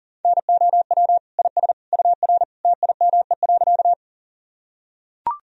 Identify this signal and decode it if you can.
Morse code, heard as a keyed tone, about 37 wpm, 700 Hz. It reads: NOW IS UR TIME.
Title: Sentences from Top 100 Words and Abbreviations 40wpm